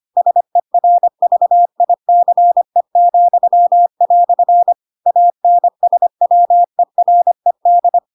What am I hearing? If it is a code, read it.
Morse code, heard as a keyed tone, about 25 wpm, 700 Hz. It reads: SERVICE," ANSWERED
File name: Pinocchio / Pinocchio ch2 25 25 ik1hge